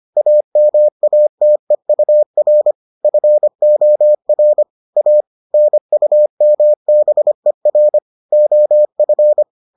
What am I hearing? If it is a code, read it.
Morse code, heard as a keyed tone, about 25 wpm, 600 Hz. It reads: AMATEUR FOR A NUMBER OF